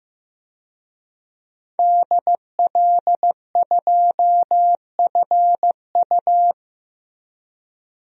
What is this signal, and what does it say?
Morse code, heard as a keyed tone, about 15 wpm, 700 Hz. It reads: DL2FU
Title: DXCC-100 Call Signs - 5 Characters - Repeated 2x - 15wpm